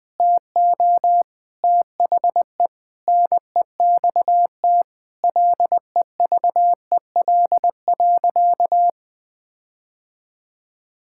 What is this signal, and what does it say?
Morse code, heard as a keyed tone, about 21 wpm, 700 Hz. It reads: TO THE NEXT LEVEL.
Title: Sentences with 7-Word Phrases - 20wpm